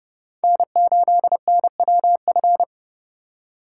Morse code, heard as a keyed tone, about 29 wpm, 700 Hz. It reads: N8NWF